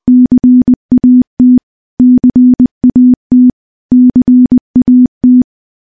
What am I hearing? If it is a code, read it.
Morse code, heard as a keyed tone, about 20 wpm, 250 Hz. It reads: CAT CAT CAT
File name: morse-training-20WPM